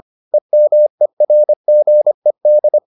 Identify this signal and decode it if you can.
Morse code, heard as a keyed tone, about 25 wpm, 600 Hz. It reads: EMERGED